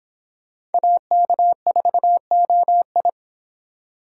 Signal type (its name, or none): Morse code